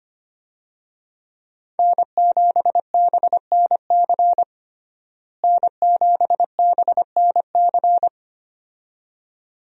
Morse code, heard as a keyed tone, about 25 wpm, 700 Hz. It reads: N7BNC N7BNC
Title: USA Call Signs - 5 Characters - Repeated 2x - 25wpm